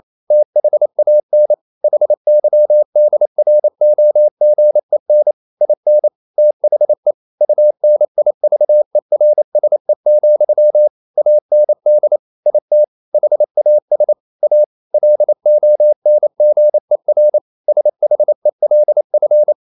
Morse code, heard as a keyed tone, about 28 wpm, 600 Hz. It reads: THAN HYDROGEN IN THE UNIVERSE, AND IT HAS A LONGER SHELF
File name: CW / Practice / EnglishQuotations0228